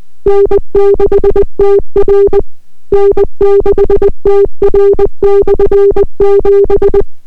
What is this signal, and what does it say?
Morse code, heard as a keyed tone, about 20 wpm, 400 Hz. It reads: N6TR N6TR/7